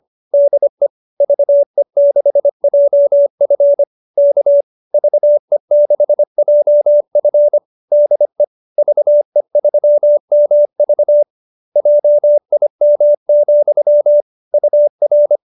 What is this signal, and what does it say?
Morse code, heard as a keyed tone, about 25 wpm, 590 Hz. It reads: DE VE6JF K VE6JF DE VE3MV JIM, UR
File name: QSO303 25